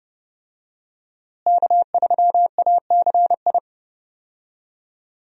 Morse code, heard as a keyed tone, about 30 wpm, 700 Hz. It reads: K3ACI
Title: USA Call Signs - 5 Characters - Repeated 2x - 30wpm